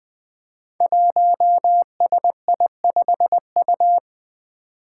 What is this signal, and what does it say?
Morse code, heard as a keyed tone, about 20 wpm, 700 Hz. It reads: 1SI5U
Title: Five Challenging Letter-Number Combinations 20wpm